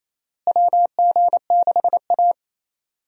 Morse code, heard as a keyed tone, about 27 wpm, 700 Hz. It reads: WG6A
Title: USA Call Signs - 4 Characters - Repeated 2x - 28wpm